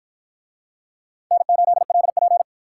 Morse code, heard as a keyed tone, about 47 wpm, 700 Hz. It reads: N8LP